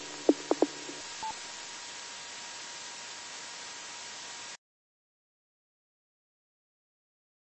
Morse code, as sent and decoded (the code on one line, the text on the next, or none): none